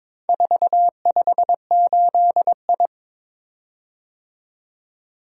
Morse code, heard as a keyed tone, about 22 wpm, 700 Hz. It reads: 458I
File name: practice / 22wpm / Four Letter-Number Combinations - 22wpm